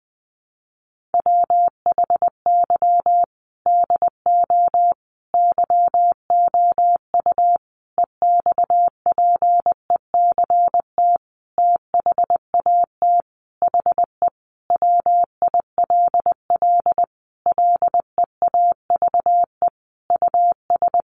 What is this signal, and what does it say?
Morse code, heard as a keyed tone, about 20 wpm, 700 Hz. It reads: WHY DO YOU EXPECT THAT HE WILL LEAVE US
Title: Sentences from Top 1100 Words 20wpm